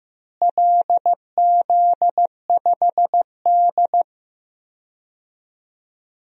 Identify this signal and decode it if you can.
Morse code, heard as a keyed tone, about 15 wpm, 700 Hz. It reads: LZ5D
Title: DXCC-100 Call Signs - 4 Characters - Repeated 2x - 15wpm